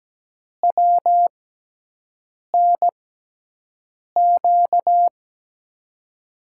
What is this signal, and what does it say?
Morse code, heard as a keyed tone, about 17 wpm, spaced as for 6 wpm, 700 Hz. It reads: WNQ
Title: USA Call Signs - 6x Character Spacing - 17wpm